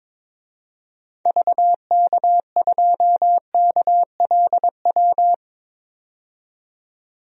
Morse code, heard as a keyed tone, about 22 wpm, 700 Hz. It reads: VK2KLW
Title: DXCC-100 Call Signs - 6 Characters - 22wpm